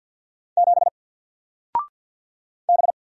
Morse code, heard as a keyed tone, about 48 wpm, 700 Hz. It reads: 7 B
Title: Single Character (Letter:Number:Punctuation:Prosign) 50wpm